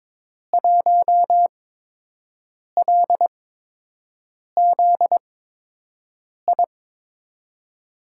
Morse code, heard as a keyed tone, about 22 wpm, spaced as for 6 wpm, 700 Hz. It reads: 1LZI
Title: USA Call Signs - 5 Characters - 8x Character Spacing - 22wpm